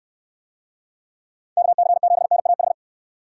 Morse code, heard as a keyed tone, about 65 wpm, 700 Hz. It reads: KC7NAH